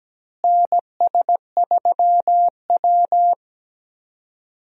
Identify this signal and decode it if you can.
Morse code, heard as a keyed tone, about 17 wpm, 700 Hz. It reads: NS3W
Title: USA Call Signs - 4 Characters - Repeated 3x - 17wpm